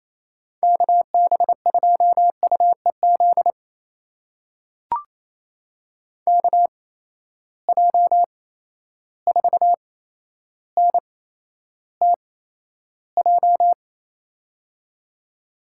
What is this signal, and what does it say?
Morse code, heard as a keyed tone, about 28 wpm, spaced as for 12 wpm, 700 Hz. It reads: KB2UEZ KJ4NTJ
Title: USA Call Signs - 6 Characters - 8x Character Spacing - 28wpm